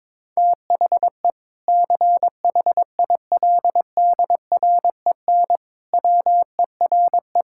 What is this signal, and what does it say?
Morse code, heard as a keyed tone, about 22 wpm, 700 Hz. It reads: THE CHILDREN WERE
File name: Difficult Sentences with Hints - 22wpm